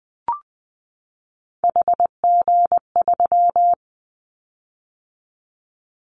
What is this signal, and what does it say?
Morse code, heard as a keyed tone, about 20 wpm, 700 Hz. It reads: HG3
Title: Callsign Prefixes for Top 25 DXCC - 20wpm